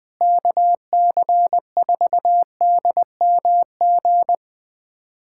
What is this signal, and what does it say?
Morse code, heard as a keyed tone, about 20 wpm, 700 Hz. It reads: KC4DMG
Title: USA Callsign DE Callsign - 6 Characters - 20wpm